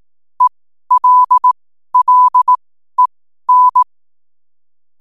Morse code, heard as a keyed tone, about 18 wpm, spaced as for 12 wpm, 1000 Hz. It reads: ELLEN
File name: n7net / 12wpm